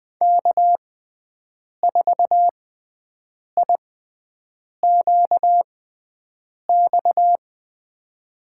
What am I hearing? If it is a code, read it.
Morse code, heard as a keyed tone, about 21 wpm, spaced as for 7 wpm, 700 Hz. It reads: K4IQX
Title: USA Call Signs - 6x Character Spacing - 20wpm